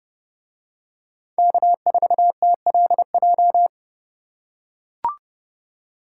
Morse code, heard as a keyed tone, about 28 wpm, 700 Hz. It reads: K4TLJ